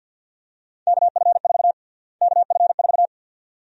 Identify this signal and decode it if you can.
Morse code, heard as a keyed tone, about 53 wpm, 700 Hz. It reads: KF4 KF4
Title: USA Call Sign Prefixes - Repeated 2x - 50wpm